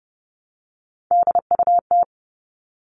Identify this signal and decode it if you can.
Morse code, heard as a keyed tone, about 30 wpm, 700 Hz. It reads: DUT
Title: All Cut Numbers - 3-Digit 30wpm